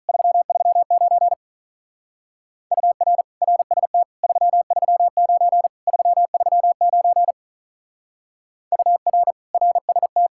Morse code, heard as a keyed tone, about 43 wpm, 700 Hz. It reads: 339 UR RST 339 339 UR RST
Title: Speed-Racing Common QSO Phrases - 35wpm